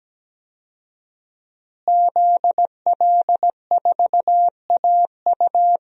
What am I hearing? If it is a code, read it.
Morse code, heard as a keyed tone, about 17 wpm, 700 Hz. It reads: ZL4AU
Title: DXCC-100 Call Signs - 5 Characters - Repeated 2x - 17wpm